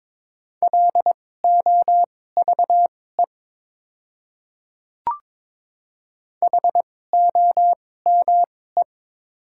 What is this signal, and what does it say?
Morse code, heard as a keyed tone, about 22 wpm, spaced as for 16 wpm, 690 Hz. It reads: LOVE HOME
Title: 2x Character Spacing - Top 500 Words - 22wpm